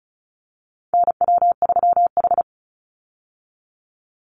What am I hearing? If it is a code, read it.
Morse code, heard as a keyed tone, about 34 wpm, 700 Hz. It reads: NW3H
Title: USA Call Signs 35wpm